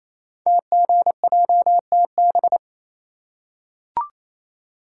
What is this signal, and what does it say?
Morse code, heard as a keyed tone, about 28 wpm, 700 Hz. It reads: TGJTB
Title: Code Groups - 5 Letters - 28wpm